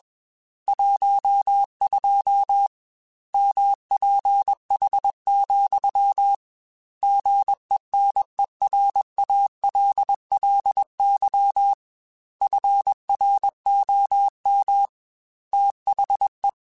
Morse code, heard as a keyed tone, about 21 wpm, 800 Hz. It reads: 12 MPH, GENERALLY FROM THE
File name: Weather Sets / Weather Sets-mp3s / Weather 5 22 @20